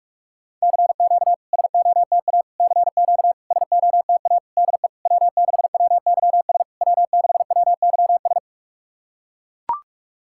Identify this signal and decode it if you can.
Morse code, heard as a keyed tone, about 41 wpm, 690 Hz. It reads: CQ SOTA CQ SOTA DE W6WYS W6WYS
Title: SOTA QSO Practice 45wpm